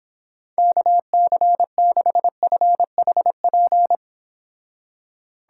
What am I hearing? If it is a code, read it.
Morse code, heard as a keyed tone, about 29 wpm, 700 Hz. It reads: KC6FHP